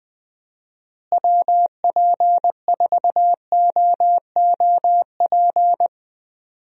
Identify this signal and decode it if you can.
Morse code, heard as a keyed tone, about 20 wpm, 700 Hz. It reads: WP4OOP
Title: USA Call Signs - 6 Characters - Repeated 3x - 20wpm